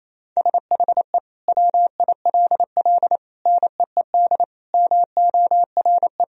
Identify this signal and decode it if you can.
Morse code, heard as a keyed tone, about 28 wpm, 700 Hz. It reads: SHE WILL NEED MORE
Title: Sentences from Top 1000 Words - Review Part 4 - 28wpm